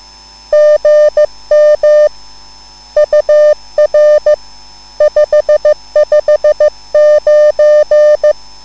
Morse code, heard as a keyed tone, about 15 wpm, 600 Hz. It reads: GM UR 559
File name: QSL / CW lesson